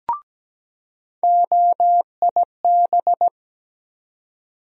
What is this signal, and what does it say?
Morse code, heard as a keyed tone, about 17 wpm, 700 Hz. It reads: OIB
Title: USA Call Sign Postfixes - 3 Letters 17wpm